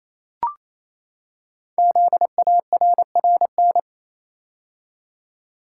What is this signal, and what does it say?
Morse code, heard as a keyed tone, about 28 wpm, 700 Hz. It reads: ZARRN
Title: Code Groups - 5 Letters - 28wpm